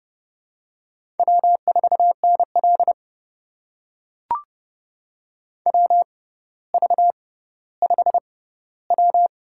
Morse code, heard as a keyed tone, about 29 wpm, spaced as for 15 wpm, 700 Hz. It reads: W4NL WV5W
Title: USA Call Signs - 4 Characters - 6x Character Spacing - 30wpm